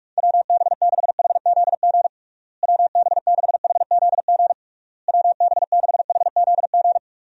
Morse code, heard as a keyed tone, about 46 wpm, 700 Hz. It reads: WB6HZG WB6HZG WB6HZG
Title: USA Call Signs - 6 Characters - Repeated 3x - 45wpm